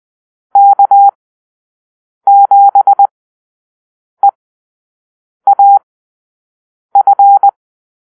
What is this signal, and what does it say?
Morse code, heard as a keyed tone, about 20 wpm, spaced as for 6 wpm, 800 Hz. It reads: K7EAF
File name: arrl / Call-Sign Drill-6